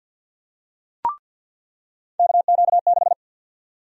Morse code, heard as a keyed tone, about 49 wpm, 700 Hz. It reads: KQ6
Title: USA Call Sign Prefixes - 50wpm